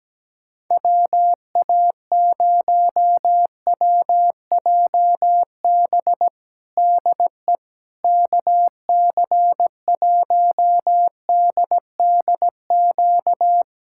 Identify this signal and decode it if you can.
Morse code, heard as a keyed tone, about 17 wpm, 700 Hz. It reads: WA0WJB DE KC1DDQ
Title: USA Callsign DE Callsign - 6 Characters - 17wpm